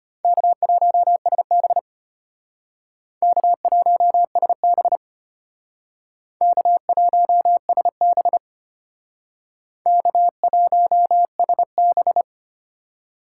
Morse code, heard as a keyed tone, about 33 wpm, 700 Hz. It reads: K1SB K1SB K1SB K1SB